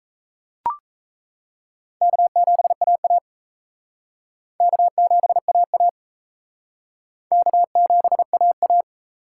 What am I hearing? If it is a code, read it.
Morse code, heard as a keyed tone, about 34 wpm, 700 Hz. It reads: K7AA K7AA K7AA
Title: Speed Racing - USA 1x2 Call Signs 28wpm